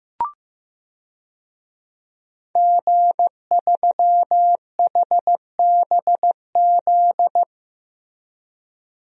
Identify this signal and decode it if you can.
Morse code, heard as a keyed tone, about 15 wpm, 700 Hz. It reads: G3HBZ
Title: DXCC-100 Call Signs - 5 Characters - Repeated 2x - 15wpm